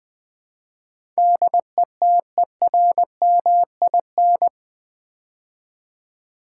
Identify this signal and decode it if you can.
Morse code, heard as a keyed tone, about 20 wpm, 700 Hz. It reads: DETERMIN
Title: Build A Word - 4 Letters At A Time - 10-Letter Words - 20wpm